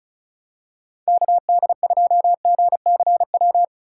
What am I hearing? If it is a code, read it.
Morse code, heard as a keyed tone, about 34 wpm, 700 Hz. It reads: KD2GCW
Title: USA Call Signs - 6 Characters - 8x Character Spacing - 35wpm